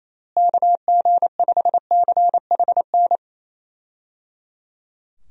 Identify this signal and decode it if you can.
Morse code, heard as a keyed tone, about 28 wpm, 700 Hz. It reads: KG5CHN
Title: USA Call Signs - 6 Characters - Repeated 3x - 28wpm